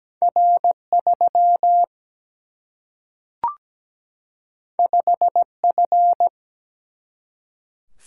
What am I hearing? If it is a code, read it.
Morse code, heard as a keyed tone, about 17 wpm, 700 Hz. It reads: R3 5F
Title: Single-Letter Single-Number 17wpm